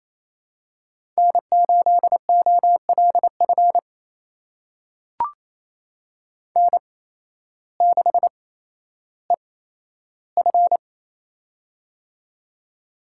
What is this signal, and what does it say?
Morse code, heard as a keyed tone, about 28 wpm, spaced as for 13 wpm, 700 Hz. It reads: N8OLF N6EF
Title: USA Call Signs - 8x Character Spacing - 28wpm